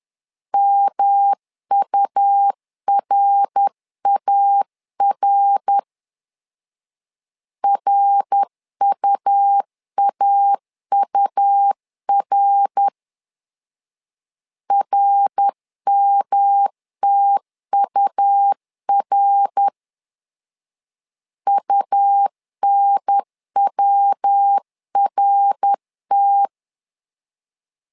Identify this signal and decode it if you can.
Morse code, heard as a keyed tone, about 10 wpm, 800 Hz. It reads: MURAR RUAUR RMTUR UNWRT
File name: LEZIONI / 3 MWRO / 3 MWR 10wpm 1952